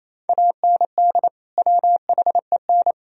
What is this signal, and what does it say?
Morse code, heard as a keyed tone, about 28 wpm, 700 Hz. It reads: AND WHEN
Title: Sentences from Top 1000 Words - Review Part 4 - 28wpm